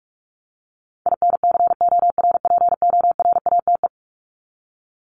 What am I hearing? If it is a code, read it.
Morse code, heard as a keyed tone, about 44 wpm, 700 Hz. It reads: INCORPORATE
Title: Top 1751-2000 Words 45wpm